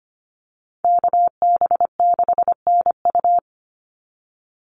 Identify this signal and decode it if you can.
Morse code, heard as a keyed tone, about 25 wpm, 700 Hz. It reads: KB6NU
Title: Call Signs - 25wpm